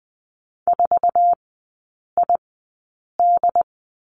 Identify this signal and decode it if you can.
Morse code, heard as a keyed tone, about 20 wpm, spaced as for 8 wpm, 700 Hz. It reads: 4ID
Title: 067 Lesson 12e - Letter d - Three Characters 20wpm